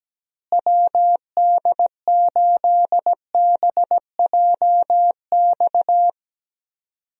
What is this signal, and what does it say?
Morse code, heard as a keyed tone, about 17 wpm, 700 Hz. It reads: WD8BJX
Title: USA Call Signs - 6 Characters - Repeated 3x - 17wpm